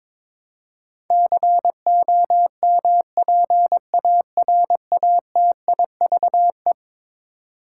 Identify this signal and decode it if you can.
Morse code, heard as a keyed tone, about 22 wpm, 700 Hz. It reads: COMPARATIVE
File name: practice / 22wpm / Eleven Letter Words - 22wpm